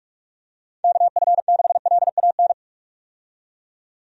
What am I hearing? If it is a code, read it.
Morse code, heard as a keyed tone, about 46 wpm, 700 Hz. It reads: KF6LAN
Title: USA Call Signs - 6 Characters - 45wpm